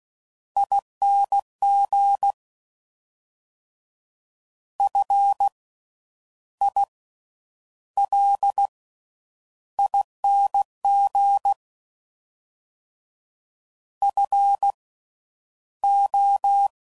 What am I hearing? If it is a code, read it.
Morse code, heard as a keyed tone, about 16 wpm, spaced as for 9 wpm, 800 Hz. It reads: ING FILING FO
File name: Slow With Faster Ings